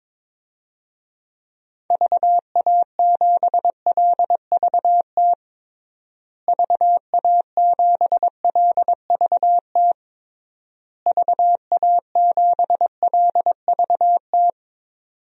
Morse code, heard as a keyed tone, about 22 wpm, 700 Hz. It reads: VA7LVT VA7LVT VA7LVT
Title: DXCC-100 Call Signs - 6 Characters - Repeated 3x - 22wpm